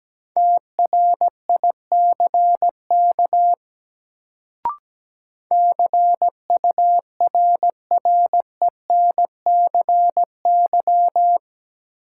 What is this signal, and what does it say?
Morse code, heard as a keyed tone, about 17 wpm, 700 Hz. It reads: TRICK CURRENCY
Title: Sentences from Top 1500 Words - New Word Drill - 17wpm